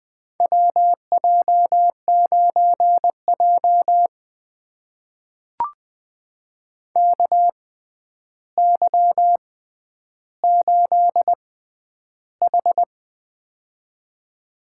Morse code, heard as a keyed tone, about 20 wpm, spaced as for 10 wpm, 700 Hz. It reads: WJ9J KY8H
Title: USA Call Signs - 4 Characters - 6x Character Spacing - 20wpm